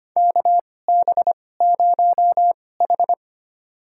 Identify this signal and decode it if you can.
Morse code, heard as a keyed tone, about 25 wpm, spaced as for 18 wpm, 700 Hz. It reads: KB0H
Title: USA Call Signs - 4 Characters - 2x Character Spacing - 25wpm